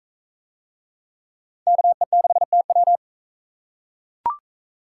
Morse code, heard as a keyed tone, about 44 wpm, 700 Hz. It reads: KE6TW